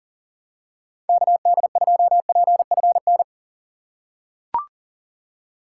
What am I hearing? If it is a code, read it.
Morse code, heard as a keyed tone, about 40 wpm, 700 Hz. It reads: KD2PFN
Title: USA Call Signs - 6 Characters - 6x Character Spacing - 40wpm